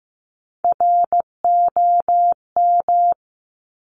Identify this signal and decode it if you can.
Morse code, heard as a keyed tone, about 15 wpm, 700 Hz. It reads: ROM
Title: Top Three Letter Combinations in Words 15wpm